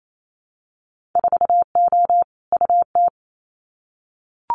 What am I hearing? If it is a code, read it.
Morse code, heard as a keyed tone, about 28 wpm, 700 Hz. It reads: 4O UT
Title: Winter Field Day Exchanges Encore 28wpm